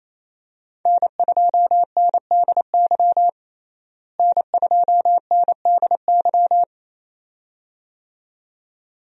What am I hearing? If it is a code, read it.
Morse code, heard as a keyed tone, about 28 wpm, 700 Hz. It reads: N2NDY N2NDY
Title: USA Call Signs - 5 Characters - Repeated 2x - 28wpm